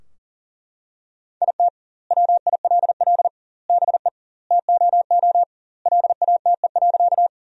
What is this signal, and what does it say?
Morse code, heard as a keyed tone, about 40 wpm, 700 Hz. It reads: IT WILL BE TOO LATE.
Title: 2x Word-Spacing - Sentences from Top 300 Words - 40wpm